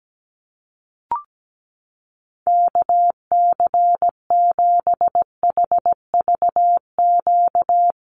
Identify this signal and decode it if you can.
Morse code, heard as a keyed tone, about 17 wpm, 700 Hz. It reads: KC7HVQ
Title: USA Call Signs Repeated 3x 17wpm